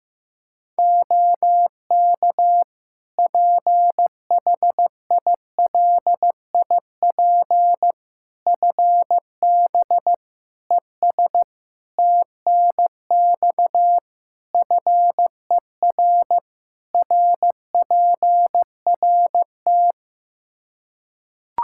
Morse code, heard as a keyed tone, about 15 wpm, 700 Hz. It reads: OK PHILIP FB ES TNX FER RPRT
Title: K1USN SST QSO Practice 15wpm